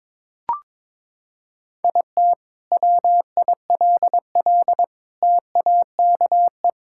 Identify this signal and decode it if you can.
Morse code, heard as a keyed tone, about 21 wpm, 700 Hz. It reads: IT WILL TAKE
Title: Sentences from Top 1000 Words - Review Part 6 - 22wpm